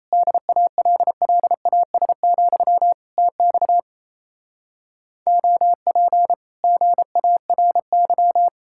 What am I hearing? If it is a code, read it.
Morse code, heard as a keyed tone, about 36 wpm, 700 Hz. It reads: DALLAS, TX OP GARY